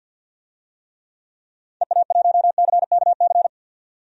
Morse code, heard as a keyed tone, about 52 wpm, 700 Hz. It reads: EA1CKC
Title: DXCC-100 Call Signs - 6 Characters - 50wpm